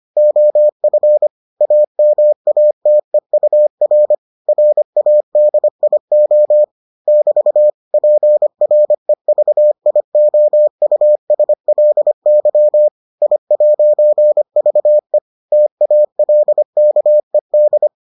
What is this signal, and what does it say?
Morse code, heard as a keyed tone, about 25 wpm, 600 Hz. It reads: OF AMATEUR RADIO <BT> PREVIOUSLY I'VE TALKED